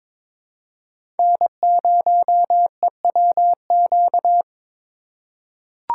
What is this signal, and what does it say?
Morse code, heard as a keyed tone, about 23 wpm, 700 Hz. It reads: N0EWQ
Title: USA Call Signs - 5 Characters - 4x Character Spacing - 22wpm